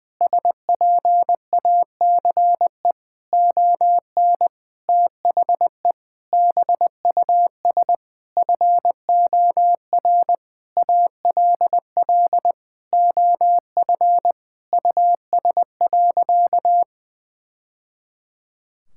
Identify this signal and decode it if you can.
Morse code, heard as a keyed tone, about 19 wpm, 700 Hz. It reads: SPACE ON THE BUS FOR ALL OF US.
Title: Sentences from Top 1000 Words - Review Part 6 - 20wpm